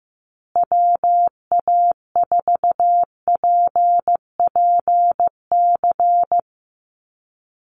Morse code, heard as a keyed tone, about 15 wpm, 700 Hz. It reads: WA4PPC
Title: USA Call Signs Repeated 2x 15wpm